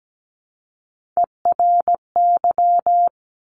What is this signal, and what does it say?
Morse code, heard as a keyed tone, about 17 wpm, 700 Hz. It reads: ERY
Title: Top Three Letter Combinations in Words 17wpm